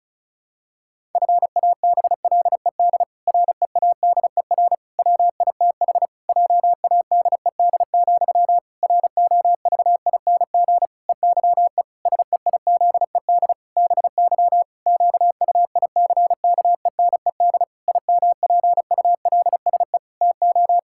Morse code, heard as a keyed tone, about 36 wpm, 700 Hz. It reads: FABLED READER WITH JADED, ROVING EYE SEIZED BY QUICKENED IMPULSE TO
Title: Pangram Challenge v2 - 35wpm